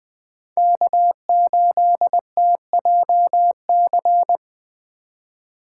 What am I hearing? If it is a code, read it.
Morse code, heard as a keyed tone, about 20 wpm, 700 Hz. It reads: K8TJC
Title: Speed Racing - USA Call Signs 17wpm